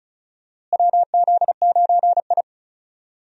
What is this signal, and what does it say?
Morse code, heard as a keyed tone, about 34 wpm, 700 Hz. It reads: WZ9I
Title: USA Call Signs - 4 Characters - Repeated 3x - 35wpm